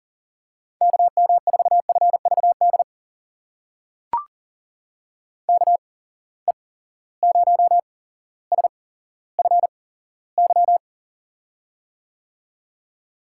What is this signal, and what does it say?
Morse code, heard as a keyed tone, about 38 wpm, spaced as for 17 wpm, 700 Hz. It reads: KM4FVD KE0SFY
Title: USA Call Signs - 6 Characters - 8x Character Spacing - 40wpm